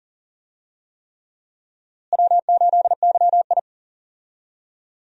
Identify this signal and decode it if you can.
Morse code, heard as a keyed tone, about 40 wpm, 700 Hz. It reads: W8YI
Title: USA Call Signs - 4 Characters - Repeated 3x - 40wpm